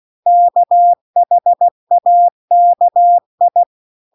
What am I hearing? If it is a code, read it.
Morse code, heard as a keyed tone, about 16 wpm, 700 Hz. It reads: KHAKI
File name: CW-The-Big-Twoheaded-River-2-16WPM